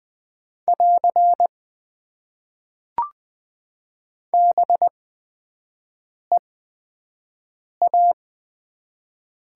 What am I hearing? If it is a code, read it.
Morse code, heard as a keyed tone, about 20 wpm, spaced as for 5 wpm, 700 Hz. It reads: <AR> BEA
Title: Top 90 QSO Elements - 8x Character Spacing - 20wpm